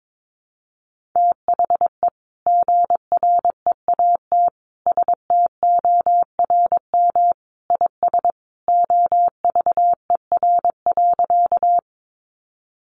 Morse code, heard as a keyed tone, about 22 wpm, 700 Hz. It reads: THE GREAT STORM IS OVER.
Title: Sentences from Top 1500 Words 22wpm